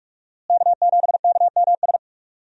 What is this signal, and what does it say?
Morse code, heard as a keyed tone, about 46 wpm, 700 Hz. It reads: K7KMS
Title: USA Call Signs - 5 Characters - Repeated 3x - 45wpm